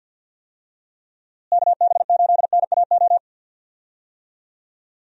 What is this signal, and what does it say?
Morse code, heard as a keyed tone, about 51 wpm, 700 Hz. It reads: KB8NAO
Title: USA Call Signs - 6 Characters - Repeated 2x - 50wpm